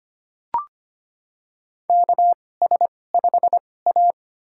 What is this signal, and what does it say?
Morse code, heard as a keyed tone, about 25 wpm, spaced as for 18 wpm, 700 Hz. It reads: KS5A
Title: USA Call Signs - 4 Characters - 2x Character Spacing - 25wpm